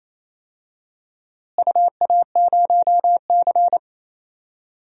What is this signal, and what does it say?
Morse code, heard as a keyed tone, about 28 wpm, 700 Hz. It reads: UA0C
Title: DXCC-100 Call Signs - 4 Characters - Repeated 2x - 28wpm